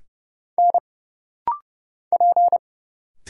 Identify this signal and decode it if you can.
Morse code, heard as a keyed tone, about 29 wpm, 700 Hz. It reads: N P
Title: Single Letters - Rapid-Fire 30wpm